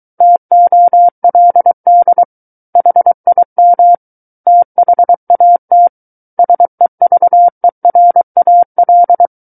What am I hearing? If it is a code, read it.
Morse code, heard as a keyed tone, about 23 wpm, 700 Hz. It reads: TOLD HIM THAT SEVERAL
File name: Book One / 14-Chapter 14 - 23 WPM